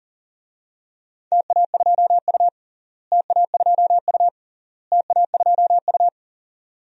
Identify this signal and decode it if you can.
Morse code, heard as a keyed tone, about 40 wpm, 700 Hz. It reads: TA2U TA2U TA2U
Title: DXCC-100 Call Signs - 4 Characters - Repeated 3x - 40wpm